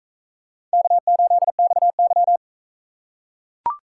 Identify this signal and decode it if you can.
Morse code, heard as a keyed tone, about 43 wpm, 700 Hz. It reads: K8XY